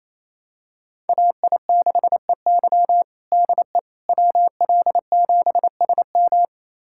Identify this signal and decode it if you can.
Morse code, heard as a keyed tone, about 28 wpm, 700 Hz. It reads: AI6EY DE WL7SM
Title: USA Callsign DE Callsign - 5 Characters - 28wpm